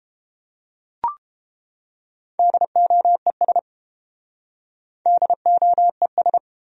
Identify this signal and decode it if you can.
Morse code, heard as a keyed tone, about 32 wpm, 700 Hz. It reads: DOES DOES